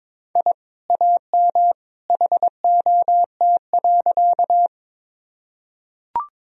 Morse code, heard as a keyed tone, about 22 wpm, 700 Hz. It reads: I AM HOT.
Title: Sentences from Top 1000 Words - Review Part 3 - 22wpm